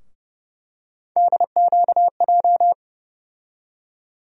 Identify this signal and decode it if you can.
Morse code, heard as a keyed tone, about 30 wpm, 700 Hz. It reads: DQJ
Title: USA Call Sign Postfixes - 3 Letters 30wpm